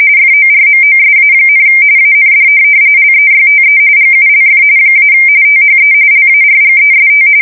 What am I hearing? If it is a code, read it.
RTTY, about 45.45 baud. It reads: FORMOSA, AND LEYTE GULF - THE LARGEST
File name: fsk170 / USS-New-Jersey